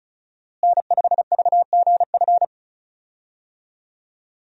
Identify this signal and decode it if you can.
Morse code, heard as a keyed tone, about 36 wpm, 700 Hz. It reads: N5VGF
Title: USA Call Signs - 5 Characters - Repeated 3x - 35wpm